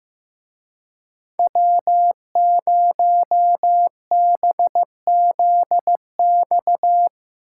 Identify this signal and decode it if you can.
Morse code, heard as a keyed tone, about 17 wpm, 700 Hz. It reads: W0BZX
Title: Speed Racing - USA 1x3 Call Signs 15wpm